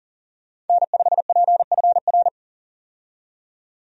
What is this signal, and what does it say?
Morse code, heard as a keyed tone, about 40 wpm, 700 Hz. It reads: N5PFR
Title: USA Call Signs - 5 Characters - 40wpm